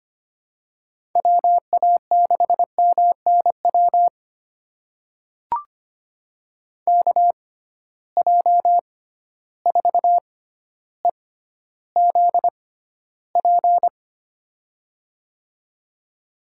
Morse code, heard as a keyed tone, about 25 wpm, spaced as for 13 wpm, 700 Hz. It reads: WA6MNW KJ4EZP
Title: USA Call Signs - 6 Characters - 6x Character Spacing - 25wpm